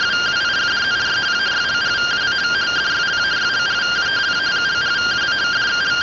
Two-tone teleprinter signal, about 170 Hz shift.